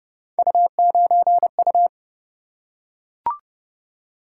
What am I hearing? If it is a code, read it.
Morse code, heard as a keyed tone, about 29 wpm, 700 Hz. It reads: U9U